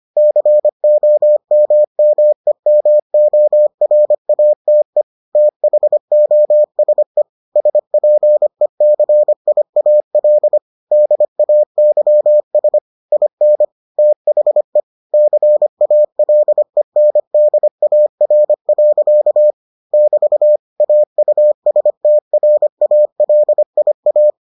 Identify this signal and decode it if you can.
Morse code, heard as a keyed tone, about 25 wpm, 600 Hz. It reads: COMMEMORATE THOSE SPECIAL DAYS IN THE CALENDAR. <BT> AUSTRALIA